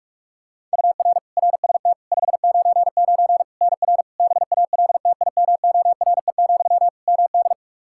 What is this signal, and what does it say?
Morse code, heard as a keyed tone, about 46 wpm, 700 Hz. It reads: UR RST 599 NR BALTIMORE, MD
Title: Practice QSO - Exchange 1 - Beginning - 45wpm